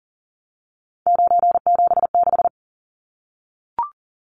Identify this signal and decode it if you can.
Morse code, heard as a keyed tone, about 40 wpm, 700 Hz. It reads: 976
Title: Numbers - 3-Digit 40wpm